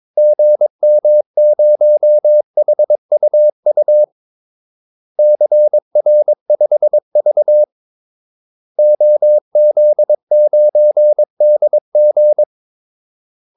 Morse code, heard as a keyed tone, about 22 wpm, 600 Hz. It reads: GM0HUU CR5V OZ9DG
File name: Callsigns-22wpm